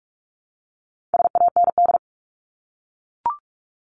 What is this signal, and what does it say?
Morse code, heard as a keyed tone, about 44 wpm, 700 Hz. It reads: SAND
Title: Top 1501-1750 Words 45wpm